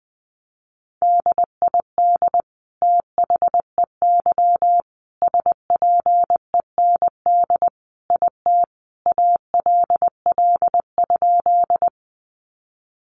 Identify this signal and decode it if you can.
Morse code, heard as a keyed tone, about 20 wpm, 700 Hz. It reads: DID THEY SPEND IT ALL?
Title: Sentences from Top 800 Words 20wpm